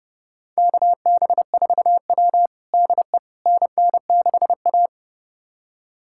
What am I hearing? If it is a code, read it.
Morse code, heard as a keyed tone, about 32 wpm, 700 Hz. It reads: KB4W DE NN6A
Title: USA Callsign DE Callsign - 4 Characters - 30wpm